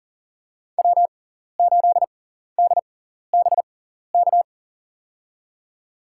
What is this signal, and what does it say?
Morse code, heard as a keyed tone, about 40 wpm, spaced as for 14 wpm, 700 Hz. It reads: W8DBK
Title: USA Call Signs - 5 Characters - 6x Character Spacing - 40wpm